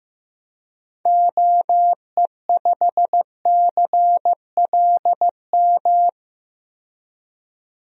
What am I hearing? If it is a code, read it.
Morse code, heard as a keyed tone, about 16 wpm, 700 Hz. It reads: OE5CLM
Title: DXCC-100 Call Signs - 6 Characters - 15wpm